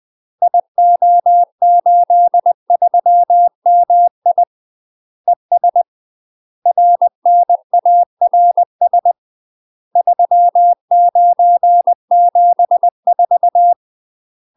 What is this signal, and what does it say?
Morse code, heard as a keyed tone, about 20 wpm, 700 Hz. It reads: IO83MI ES RNARS 3974
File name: UK MASTER QSOs TRAINING / 20WPM / UK MASTER QSOs 0031